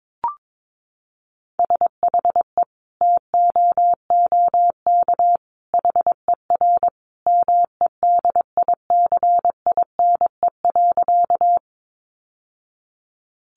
Morse code, heard as a keyed tone, about 22 wpm, 700 Hz. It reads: SHE TOOK HER MEDICINE.
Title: Sentences from Top 900 Words 22wpm